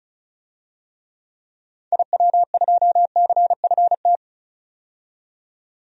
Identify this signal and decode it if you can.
Morse code, heard as a keyed tone, about 34 wpm, 700 Hz. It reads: IW2CFT
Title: DXCC-100 Call Signs - 6 Characters - 35wpm